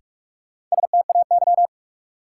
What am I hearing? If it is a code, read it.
Morse code, heard as a keyed tone, about 46 wpm, 700 Hz. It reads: STAY
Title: Sentences from Top 500 Words - New Word Drill - 45wpm